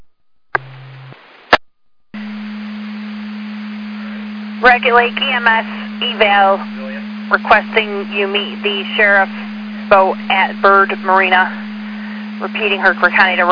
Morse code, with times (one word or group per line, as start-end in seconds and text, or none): none